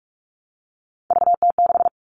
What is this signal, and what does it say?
Morse code, heard as a keyed tone, about 44 wpm, 700 Hz. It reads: VT6
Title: All Cut Numbers - 3-Digit 45wpm